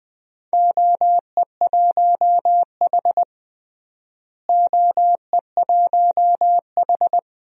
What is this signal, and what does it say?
Morse code, heard as a keyed tone, about 20 wpm, 700 Hz. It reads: OE1H OE1H
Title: DXCC-100 Call Signs - 4 Characters - Repeated 3x - 20wpm